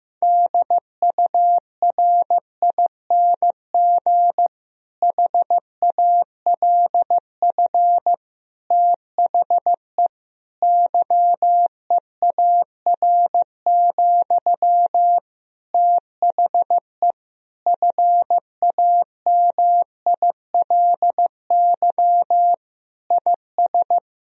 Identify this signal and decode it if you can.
Morse code, heard as a keyed tone, about 15 wpm, 700 Hz. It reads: DURING HALF THE YEAR, THE FAMILY IS
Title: Sentences from Top 1000 Words - Review Part 2 - 15wpm